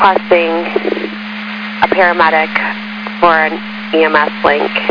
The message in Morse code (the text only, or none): none